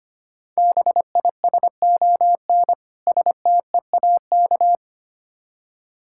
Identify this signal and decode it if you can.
Morse code, heard as a keyed tone, about 25 wpm, 700 Hz. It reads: BISON STEAK
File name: practice / 25wpm / Difficult Words with Hints - 25wpm